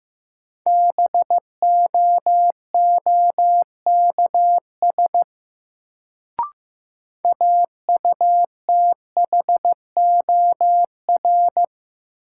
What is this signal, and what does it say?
Morse code, heard as a keyed tone, about 15 wpm, 700 Hz. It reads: BOOKS AUTHOR
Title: Sentences from Top 1200 Words - New Word Drill Alt - 15wpm